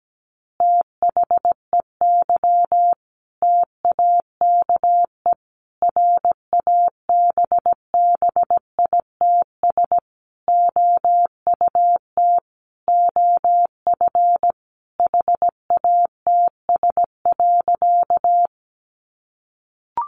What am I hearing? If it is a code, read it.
Morse code, heard as a keyed tone, about 17 wpm, 700 Hz. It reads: THEY TAKE RABBITS OUT OF HATS.
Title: Easy Phrases 17wpm